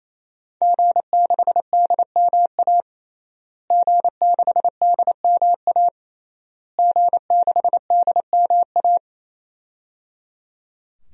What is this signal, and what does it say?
Morse code, heard as a keyed tone, about 28 wpm, 700 Hz. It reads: G6DMA G6DMA G6DMA
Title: DXCC-100 Call Signs - 5 Characters - Repeated 3x - 28wpm